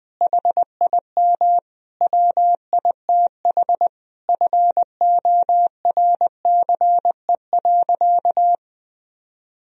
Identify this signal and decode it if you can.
Morse code, heard as a keyed tone, about 20 wpm, 700 Hz. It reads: HIM WITH FORCE.
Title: Sentences from Top 1000 Words - Review Part 3 - 20wpm